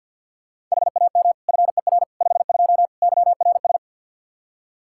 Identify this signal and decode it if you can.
Morse code, heard as a keyed tone, about 48 wpm, 700 Hz. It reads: HAM FER 52 YRS
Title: Practice QSO - Exchange 3 - More Info - 50wpm